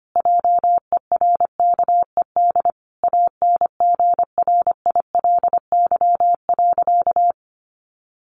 Morse code, heard as a keyed tone, about 25 wpm, 700 Hz. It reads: JERKED ANGRILY.
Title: A Sound Of Thunder 25wpm